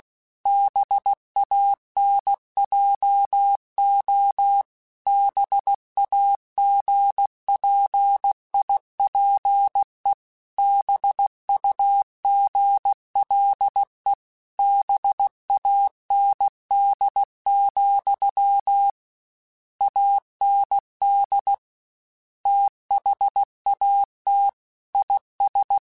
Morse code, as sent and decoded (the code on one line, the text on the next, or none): -... .- -. .--- --- / -... .- --. .--. .. .--. / -... ..- --. .-.. / -... .- -. -.. --..-- / .- -. -.. / - .... .- - / .. ...
BANJO BAGPIPE BUGLE BAND, AND THAT IS